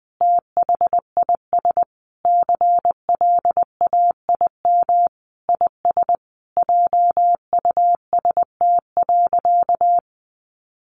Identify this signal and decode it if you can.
Morse code, heard as a keyed tone, about 20 wpm, 700 Hz. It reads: THIS CLAIM IS JUST.
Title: Sentences from Top 1500 Words 20wpm